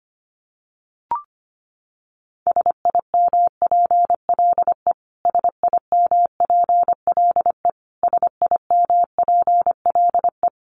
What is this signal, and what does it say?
Morse code, heard as a keyed tone, about 25 wpm, 700 Hz. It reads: SIMPLE SIMPLE SIMPLE
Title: Top 500 Words Repeated 25wpm